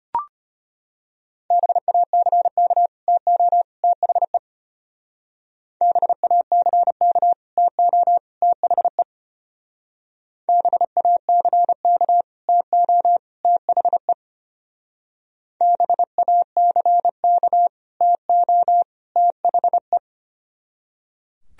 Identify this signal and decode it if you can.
Morse code, heard as a keyed tone, about 31 wpm, 700 Hz. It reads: BACK TO THE BACK TO THE BACK TO THE BACK TO THE
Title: Speed-Racing - Sets of 3 Words 25wpm